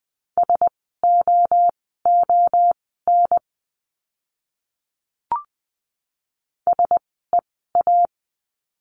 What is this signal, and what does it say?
Morse code, heard as a keyed tone, about 20 wpm, spaced as for 14 wpm, 700 Hz. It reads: SOON SEA
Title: 036 Lesson 7f - Letter s - Words 20wpm